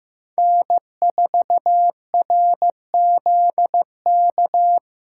Morse code, heard as a keyed tone, about 18 wpm, 700 Hz. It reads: N4RZK
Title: Speed Racing - USA 1x3 Call Signs 15wpm